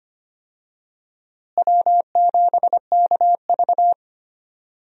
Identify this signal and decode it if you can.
Morse code, heard as a keyed tone, about 25 wpm, 700 Hz. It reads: W7KV
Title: USA Call Signs - 4 Characters - Repeated 2x - 25wpm